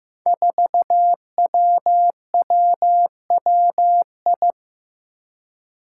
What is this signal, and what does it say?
Morse code, heard as a keyed tone, about 15 wpm, 700 Hz. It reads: 4WWWI
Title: Five Challenging Letter-Number Combinations 15wpm